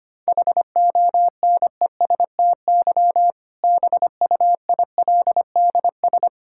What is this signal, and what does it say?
Morse code, heard as a keyed tone, about 25 wpm, 700 Hz. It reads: HONESTY BUILDS
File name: practice / 25wpm / Difficult Sentences with Hints - 25wpm